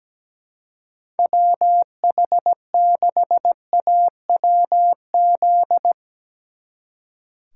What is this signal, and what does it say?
Morse code, heard as a keyed tone, about 18 wpm, 700 Hz. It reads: WH6AWZ
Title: Speed Racing - USA 2x3 Call Signs 17wpm